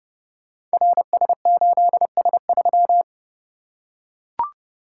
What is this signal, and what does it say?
Morse code, heard as a keyed tone, about 30 wpm, 700 Hz. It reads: RS8S3
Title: Code Groups - 5 Letter:Numbers - 30wpm